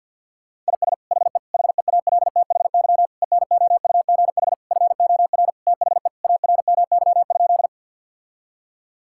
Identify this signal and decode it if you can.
Morse code, heard as a keyed tone, about 48 wpm, 690 Hz. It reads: IS HE HEALTHY ENOUGH FOR THE ARMY?
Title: Sentences from Top 1000 Words - Review Part 5 - 50wpm